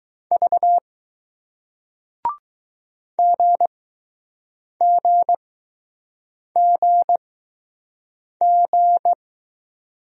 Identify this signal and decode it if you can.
Morse code, heard as a keyed tone, about 20 wpm, spaced as for 6 wpm, 700 Hz. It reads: V GGGG